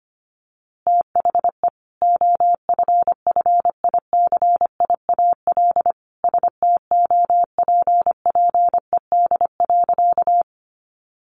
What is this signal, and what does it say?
Morse code, heard as a keyed tone, about 25 wpm, 700 Hz. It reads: THE OFFICIAL STOPPED.
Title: A Sound Of Thunder 25wpm